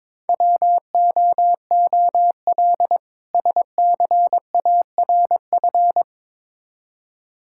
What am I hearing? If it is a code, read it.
Morse code, heard as a keyed tone, about 22 wpm, 700 Hz. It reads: WOOL SCARF
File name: Difficult Words with Hints - 22wpm